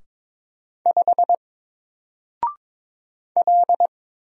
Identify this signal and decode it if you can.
Morse code, heard as a keyed tone, about 22 wpm, 700 Hz. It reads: 5 L
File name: Single-Letter-Number-22wpm